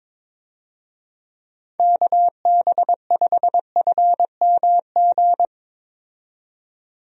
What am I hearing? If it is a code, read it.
Morse code, heard as a keyed tone, about 22 wpm, 700 Hz. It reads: KB5FMG
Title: USA Call Signs - 6 Characters - 22wpm